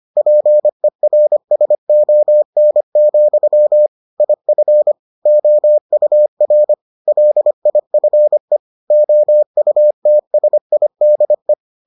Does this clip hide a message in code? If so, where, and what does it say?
Yes: Morse code from 0.2 to 11.5 s, heard as a keyed tone, about 25 wpm, 600 Hz. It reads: PERSON, IF OUR LIFE OUTSIDE